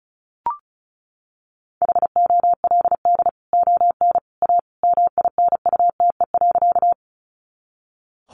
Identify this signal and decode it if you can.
Morse code, heard as a keyed tone, about 34 wpm, 700 Hz. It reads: HOLD ON A MINUTE.
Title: Sentences from Top 500 Words 35wpm